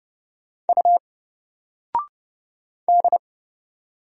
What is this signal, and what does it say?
Morse code, heard as a keyed tone, about 31 wpm, 700 Hz. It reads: U D
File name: Single-Letter-Number-30wpm